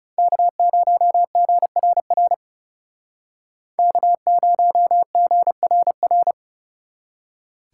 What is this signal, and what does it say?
Morse code, heard as a keyed tone, about 35 wpm, 700 Hz. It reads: K0GRR K0GRR
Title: Speed Racing - USA 1x3 Call Signs 30wpm